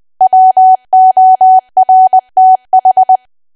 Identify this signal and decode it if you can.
Morse code, heard as a keyed tone, about 20 wpm, 750 Hz. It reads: WORTH
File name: Archive / 20 WPM / 20WPM